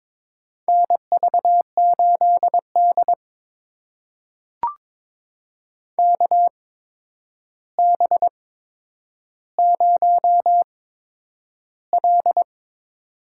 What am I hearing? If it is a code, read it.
Morse code, heard as a keyed tone, about 22 wpm, spaced as for 9 wpm, 700 Hz. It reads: NV8D KB0L
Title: USA Call Signs - 4 Characters - 8x Character Spacing - 22wpm